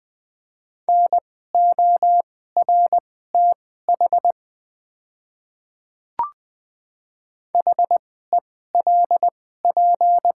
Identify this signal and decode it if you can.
Morse code, heard as a keyed tone, about 20 wpm, spaced as for 15 wpm, 700 Hz. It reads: NORTH HELP
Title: 2x Character Spacing - Top 300 Words - 20wpm